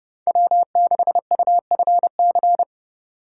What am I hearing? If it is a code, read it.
Morse code, heard as a keyed tone, about 29 wpm, 700 Hz. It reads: W6UFC